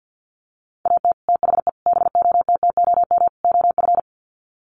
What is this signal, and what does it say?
Morse code, heard as a keyed tone, about 48 wpm, 700 Hz. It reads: AT THE BOTTOM OF
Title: Sets of 4 Words 50wpm